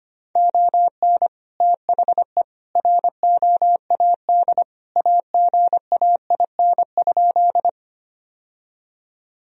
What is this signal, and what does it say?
Morse code, heard as a keyed tone, about 25 wpm, 700 Hz. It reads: ON THE ROAD AGAIN?
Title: Sentences from Top 1000 Words - Review Part 2 - 25wpm